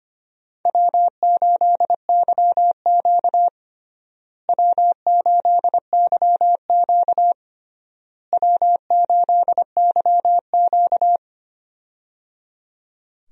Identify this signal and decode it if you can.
Morse code, heard as a keyed tone, about 25 wpm, 700 Hz. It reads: W8YQ W8YQ W8YQ
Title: USA Call Signs - 4 Characters - Repeated 3x - 25wpm